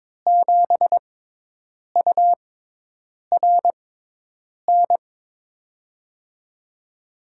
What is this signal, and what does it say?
Morse code, heard as a keyed tone, about 21 wpm, spaced as for 8 wpm, 700 Hz. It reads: 7URN
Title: USA Call Signs - 6x Character Spacing - 22wpm